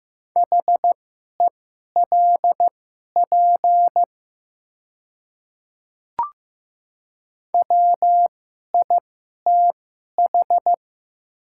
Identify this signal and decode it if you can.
Morse code, heard as a keyed tone, about 15 wpm, spaced as for 11 wpm, 700 Hz. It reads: HELP WITH
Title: 2x Character Spacing - Top 500 Words - 15wpm